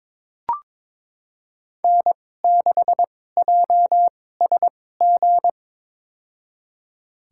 Morse code, heard as a keyed tone, about 22 wpm, spaced as for 16 wpm, 700 Hz. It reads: N6JSG
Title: USA Call Signs - 2x Character Spacing - 22wpm